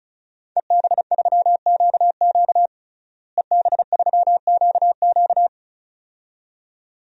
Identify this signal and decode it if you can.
Morse code, heard as a keyed tone, about 36 wpm, 700 Hz. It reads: EB3QQ EB3QQ
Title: DXCC-100 Call Signs - 5 Characters - Repeated 2x - 35wpm